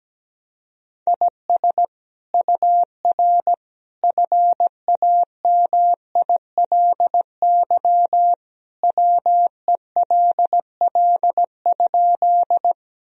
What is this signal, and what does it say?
Morse code, heard as a keyed tone, about 17 wpm, 700 Hz. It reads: IS UR FAMILY WELL?
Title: Sentences from Top 400 Words and Abbreviations 17wpm